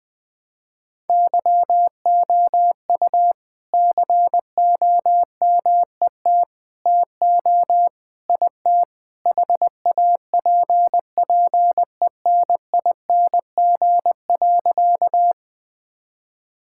Morse code, heard as a keyed tone, about 20 wpm, 700 Hz. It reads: YOU COMET TO IT HAPPENING.
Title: All-Age Appropriate Jokes - 20wpm